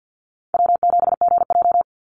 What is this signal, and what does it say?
Morse code, heard as a keyed tone, about 48 wpm, 700 Hz. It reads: W7GJ
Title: NA SOTA Chaser Callsigns 50wpm